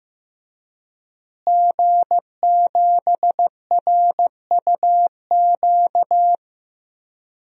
Morse code, heard as a keyed tone, about 15 wpm, 700 Hz. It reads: G7RUQ
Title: DXCC-100 Call Signs - 5 Characters - Repeated 2x - 15wpm